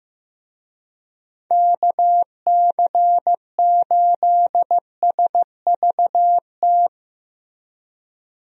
Morse code, heard as a keyed tone, about 15 wpm, 700 Hz. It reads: KC8SVT
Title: USA Call Signs - 6 Characters - Repeated 3x - 15wpm